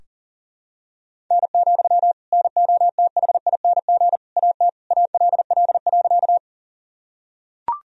Morse code, heard as a keyed tone, about 40 wpm, 700 Hz. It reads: N, NOTHING AT ALL.
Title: Sentences from Top 500 Words and Abbreviations 40wpm